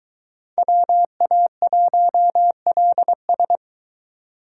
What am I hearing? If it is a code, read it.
Morse code, heard as a keyed tone, about 23 wpm, 700 Hz. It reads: WA1LS